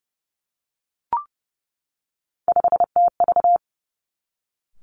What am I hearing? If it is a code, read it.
Morse code, heard as a keyed tone, about 30 wpm, 700 Hz. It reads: 5TV
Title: Three Challenging Letter-Number Combinations 30wpm